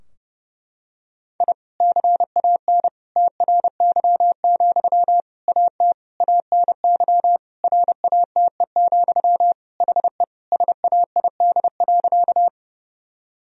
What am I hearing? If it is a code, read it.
Morse code, heard as a keyed tone, about 30 wpm, 700 Hz. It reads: I CAN TRY, AT ANY RATE, HE SAID.
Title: Sentences from Top 1000 Words - Review Part 4 - 30wpm